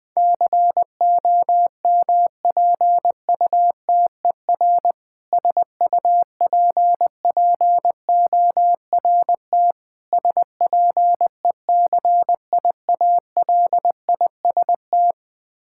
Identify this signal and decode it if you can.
Morse code, heard as a keyed tone, about 20 wpm, 700 Hz. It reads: COMPUTER SUPPORT SPECIALIST
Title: Practice QSO - Exchange 3 - More Info - 20wpm